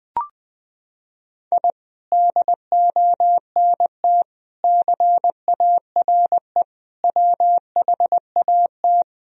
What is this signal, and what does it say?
Morse code, heard as a keyed tone, about 21 wpm, 700 Hz. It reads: I DONT CARE WHAT
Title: Sentences from Top 1000 Words - Review Part 1 - 20wpm